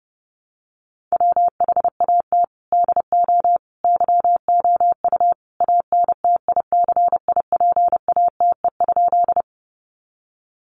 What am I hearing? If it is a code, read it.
Morse code, heard as a keyed tone, about 30 wpm, 700 Hz. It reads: WHAT DO YOU ANTICIPATE?
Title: Sentences from Top 1500 Words 30wpm